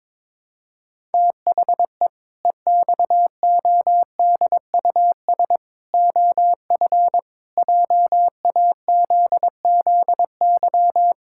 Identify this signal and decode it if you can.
Morse code, heard as a keyed tone, about 22 wpm, 700 Hz. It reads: THE EXODUS OF JAZZY
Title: Pangram Challenge v2 - 22wpm